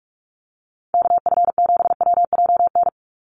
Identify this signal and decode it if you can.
Morse code, heard as a keyed tone, about 44 wpm, 700 Hz. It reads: KF7WJN
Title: USA Call Signs 45wpm